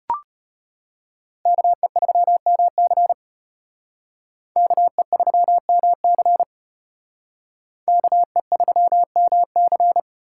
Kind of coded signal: Morse code

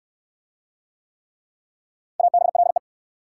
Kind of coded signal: Morse code

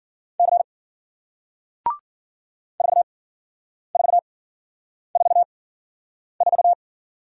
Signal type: Morse code

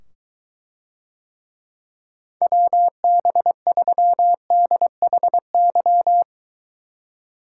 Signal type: Morse code

